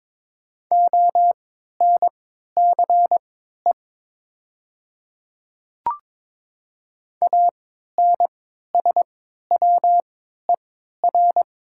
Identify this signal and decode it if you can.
Morse code, heard as a keyed tone, about 22 wpm, spaced as for 12 wpm, 700 Hz. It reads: ONCE ANSWER
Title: 3x Character Spacing - Top 300 Words - 22wpm